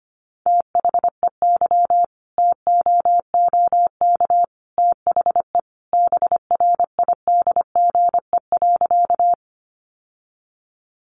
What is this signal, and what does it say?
Morse code, heard as a keyed tone, about 25 wpm, 700 Hz. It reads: THEY TOOK THE BRIDGE.
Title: Sentences from Top 1200 Words 25wpm